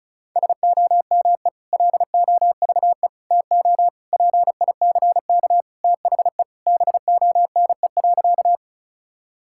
Morse code, heard as a keyed tone, about 36 wpm, 700 Hz. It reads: SOME LOVE TO PICK THE BONE.
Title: Sentences from Top 1600 Words 35wpm